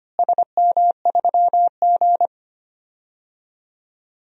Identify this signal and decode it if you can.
Morse code, heard as a keyed tone, about 25 wpm, 700 Hz. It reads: SM3G
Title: DXCC-100 Call Signs - 4 Characters - Repeated 2x - 25wpm